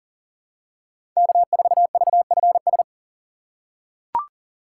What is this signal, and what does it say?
Morse code, heard as a keyed tone, about 40 wpm, 700 Hz. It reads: K4VFS
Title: USA Call Signs - 5 Characters - 2x Character Spacing - 40wpm